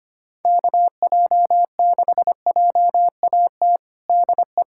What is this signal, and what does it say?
Morse code, heard as a keyed tone, about 25 wpm, 700 Hz. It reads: KJ6JAT DE
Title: USA Callsign DE Callsign - 6 Characters - 25wpm